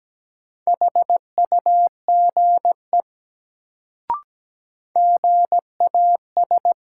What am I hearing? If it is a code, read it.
Morse code, heard as a keyed tone, about 17 wpm, 700 Hz. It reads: HUGE GAS
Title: Sentences from Top 1100 Words - New Word Drill Alt - 17wpm